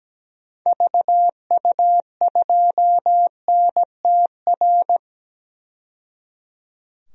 Morse code, heard as a keyed tone, about 17 wpm, 700 Hz. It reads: VU2NTR
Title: DXCC-100 Call Signs - 6 Characters - 17wpm